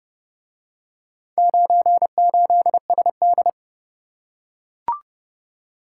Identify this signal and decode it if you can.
Morse code, heard as a keyed tone, about 29 wpm, 700 Hz. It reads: 98SD